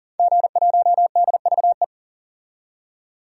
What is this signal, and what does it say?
Morse code, heard as a keyed tone, about 40 wpm, 700 Hz. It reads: G1DVE
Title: DXCC-100 Call Signs - 5 Characters - 40wpm